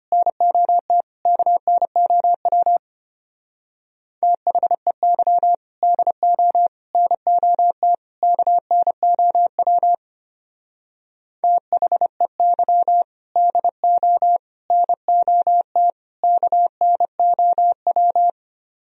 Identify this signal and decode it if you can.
Morse code, heard as a keyed tone, about 31 wpm, 700 Hz. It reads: NOT KNOW THEY DO NOT KNOW THEY DO NOT KNOW
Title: Speed-Racing - Sets of 4 Words 25wpm